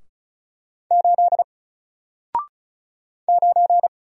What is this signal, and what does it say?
Morse code, heard as a keyed tone, about 36 wpm, 700 Hz. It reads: 8 9
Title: Single Letter-Number Alt 35wpm